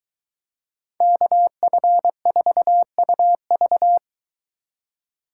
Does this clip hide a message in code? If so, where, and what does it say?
Yes: Morse code from 1.0 to 4.0 s, heard as a keyed tone, about 22 wpm, 700 Hz. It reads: KF4UV